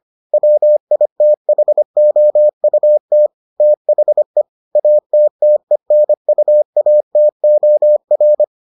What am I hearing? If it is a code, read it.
Morse code, heard as a keyed tone, about 25 wpm, 600 Hz. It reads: WITHOUT THE ATTENUATOR